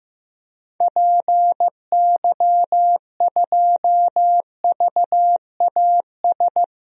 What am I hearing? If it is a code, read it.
Morse code, heard as a keyed tone, about 15 wpm, 700 Hz. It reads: PY2VAS
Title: DXCC-100 Call Signs - 6 Characters - Repeated 2x - 15wpm